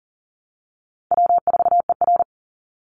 Morse code, heard as a keyed tone, about 40 wpm, 700 Hz. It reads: W4ER
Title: Easy Call Signs Repeated 3x - 40wpm